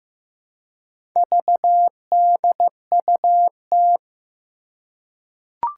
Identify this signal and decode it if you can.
Morse code, heard as a keyed tone, about 15 wpm, 700 Hz. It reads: VDUT
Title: Four Challenging Letter-Number Combinations 15wpm